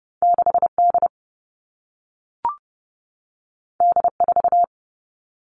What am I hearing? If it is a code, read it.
Morse code, heard as a keyed tone, about 30 wpm, 700 Hz. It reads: 6D D4
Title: All Cut Numbers - 2-Digit 30wpm